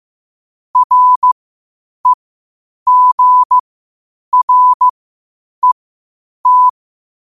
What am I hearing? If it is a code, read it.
Morse code, heard as a keyed tone, about 15 wpm, spaced as for 8 wpm, 1000 Hz. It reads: REGRET
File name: qotdfiles / 8wpm